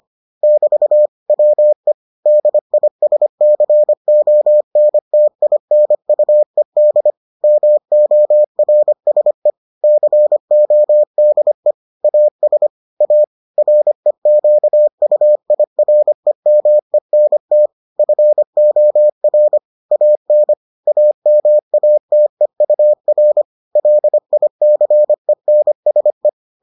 Morse code, heard as a keyed tone, about 25 wpm, 600 Hz. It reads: <BT> WE DISCONTINUED MORSE CODE AS A REQUIREMENT FOR AN AMATEUR LICENSE